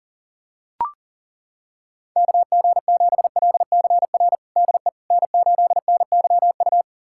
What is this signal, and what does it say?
Morse code, heard as a keyed tone, about 40 wpm, 700 Hz. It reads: KG7LCR DE N8NYU
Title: USA Callsign DE Callsign 40wpm